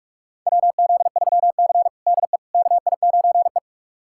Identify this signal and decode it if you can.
Morse code, heard as a keyed tone, about 45 wpm, 700 Hz. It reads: WZ3C DE KI9E
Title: USA Callsign DE Callsign - 4 Characters - 45wpm